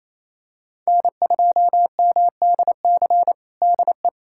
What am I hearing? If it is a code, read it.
Morse code, heard as a keyed tone, about 28 wpm, 700 Hz. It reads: N2MDC DE
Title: USA Callsign DE Callsign - 5 Characters - 28wpm